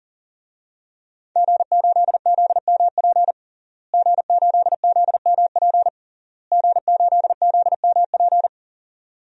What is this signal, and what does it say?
Morse code, heard as a keyed tone, about 37 wpm, 700 Hz. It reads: G8ZMP G8ZMP G8ZMP
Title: DXCC-100 Call Signs - 5 Characters - Repeated 3x - 40wpm